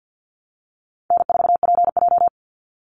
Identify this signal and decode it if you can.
Morse code, heard as a keyed tone, about 48 wpm, 700 Hz. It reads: N4PJ
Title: Call Signs Repeated 3x - 50wpm